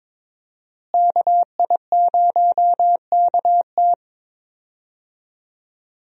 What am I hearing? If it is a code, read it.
Morse code, heard as a keyed tone, about 21 wpm, 700 Hz. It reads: KI0KT
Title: USA Call Signs - 5 Characters - Repeated 3x - 22wpm